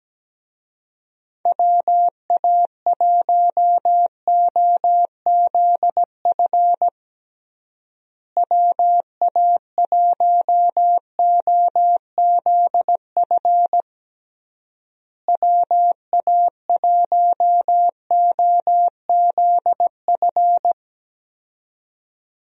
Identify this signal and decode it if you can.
Morse code, heard as a keyed tone, about 17 wpm, 700 Hz. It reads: WA1OZF WA1OZF WA1OZF
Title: USA Call Signs - 6 Characters - Repeated 3x - 17wpm